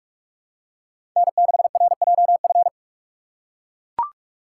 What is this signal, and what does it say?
Morse code, heard as a keyed tone, about 42 wpm, 700 Hz. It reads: N6RJF
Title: USA Call Signs - 5 Characters - 2x Character Spacing - 45wpm